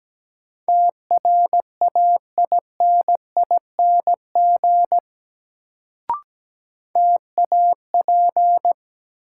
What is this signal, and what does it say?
Morse code, heard as a keyed tone, about 17 wpm, 700 Hz. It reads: TRAINING TAP
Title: Sentences from Top 1400 Words - New Word Drill Alt - 17wpm